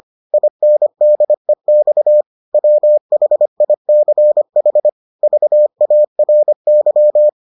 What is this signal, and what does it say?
Morse code, heard as a keyed tone, about 25 wpm, 600 Hz. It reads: INDEX WHICH VARY